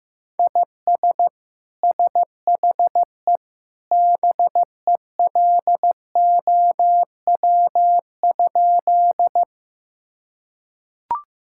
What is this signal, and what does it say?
Morse code, heard as a keyed tone, about 16 wpm, 700 Hz. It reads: IS SHE BELOW?
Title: Sentences from Top 1000 Words - Review Part 5 - 15wpm